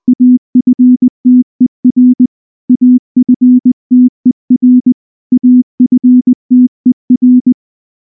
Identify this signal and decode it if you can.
Morse code, heard as a keyed tone, about 19 wpm, 250 Hz. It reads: AFTER AFTER AFTER
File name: morse / morse-training-20WPM-1